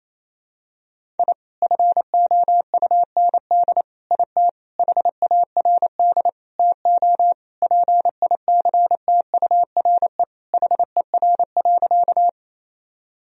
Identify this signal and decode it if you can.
Morse code, heard as a keyed tone, about 27 wpm, 700 Hz. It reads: I FOUND IT HARD TO PICTURE HER.
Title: Sentences from Top 1000 Words - Review Part 1 - 28wpm